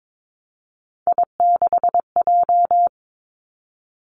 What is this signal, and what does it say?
Morse code, heard as a keyed tone, about 22 wpm, 700 Hz. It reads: I6J
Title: Three Challenging Letter-Number Combinations 22wpm